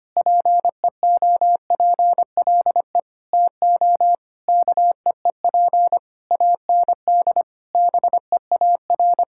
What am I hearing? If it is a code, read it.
Morse code, heard as a keyed tone, about 25 wpm, 700 Hz. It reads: PEOPLE TO KEEP AND BEAR
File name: practice / 25wpm / Sentences with 9-Word Phrases - 25wpm